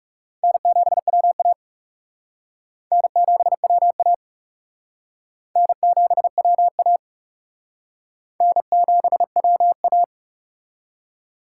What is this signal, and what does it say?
Morse code, heard as a keyed tone, about 36 wpm, 700 Hz. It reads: N7WA N7WA N7WA N7WA
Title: Speed Racing - USA 1x2 Call Signs 30wpm